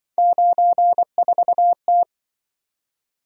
Morse code, heard as a keyed tone, about 24 wpm, 700 Hz. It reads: 94T
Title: Speed-Racing - Three Letter-Number Combinations - 20wpm